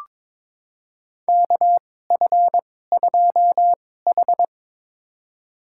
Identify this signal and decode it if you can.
Morse code, heard as a keyed tone, about 23 wpm, spaced as for 16 wpm, 700 Hz. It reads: KF2H
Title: USA Call Signs - 4 Characters - 2x Character Spacing - 22wpm